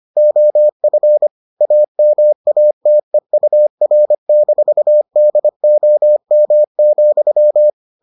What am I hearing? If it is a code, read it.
Morse code, heard as a keyed tone, about 25 wpm, 600 Hz. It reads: OF AMATEUR-DOM,